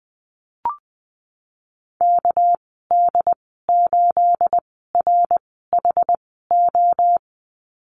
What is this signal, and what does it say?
Morse code, heard as a keyed tone, about 20 wpm, spaced as for 15 wpm, 700 Hz. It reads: KD8RHO
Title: 213 Lesson 34g - Number 8 - Call Signs 20wpm